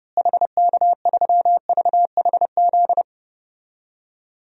Morse code, heard as a keyed tone, about 30 wpm, 700 Hz. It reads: HK3VHZ
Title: DXCC-100 Call Signs - 6 Characters - Repeated 2x - 30wpm